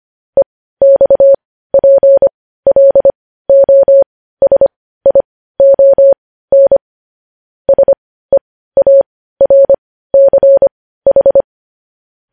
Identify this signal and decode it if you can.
Morse code, heard as a keyed tone, about 25 wpm, spaced as for 15 wpm, 550 Hz. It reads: EXPLOSION SEARCH